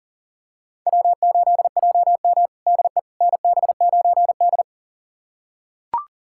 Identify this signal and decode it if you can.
Morse code, heard as a keyed tone, about 40 wpm, 700 Hz. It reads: W8JM DE NB9D
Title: USA Callsign DE Callsign - 4 Characters - 40wpm